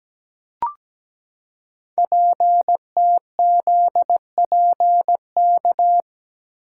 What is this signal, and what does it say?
Morse code, heard as a keyed tone, about 17 wpm, 700 Hz. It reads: PTZPK
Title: Code Groups - 5 Letters - 17wpm